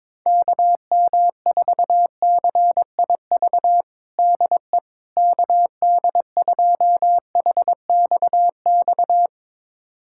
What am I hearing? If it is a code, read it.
Morse code, heard as a keyed tone, about 21 wpm, 700 Hz. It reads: KM4CIV DE KD2HXX
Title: USA Callsign DE Callsign 22wpm